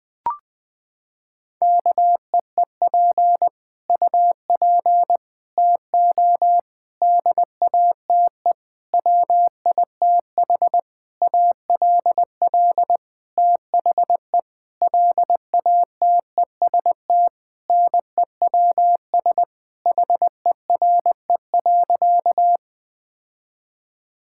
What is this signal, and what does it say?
Morse code, heard as a keyed tone, about 19 wpm, 700 Hz. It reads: KEEP UP TO DATE WITH ALL THE LATEST NEWS HERE.
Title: Sentences with 9-Word Phrases - 20wpm